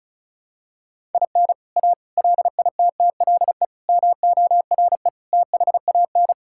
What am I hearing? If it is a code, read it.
Morse code, heard as a keyed tone, about 36 wpm, 700 Hz. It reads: IN A LITTLE MORE THAN
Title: Sentences from Top 1000 Words - Review Part 5 - 35wpm